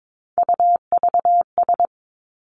Morse code, heard as a keyed tone, about 22 wpm, 700 Hz. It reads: UVS
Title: Three Challenging Letter-Number Combinations 22wpm